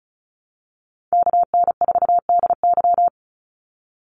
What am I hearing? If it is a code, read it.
Morse code, heard as a keyed tone, about 34 wpm, 700 Hz. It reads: KN4DY
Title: USA Call Signs 35wpm